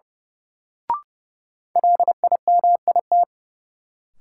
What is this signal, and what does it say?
Morse code, heard as a keyed tone, about 30 wpm, 700 Hz. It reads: LIMIT
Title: Sentences from Top 1500 Words - New Word Drill Alt - 30wpm